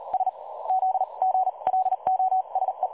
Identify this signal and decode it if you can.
Morse code, heard as a keyed tone, about 42 wpm, spaced as for 28 wpm, 760 Hz. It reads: SZ GLOSE
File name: dlazaawansowanych / nagraniazpasma / websdr recording 3565.0kHz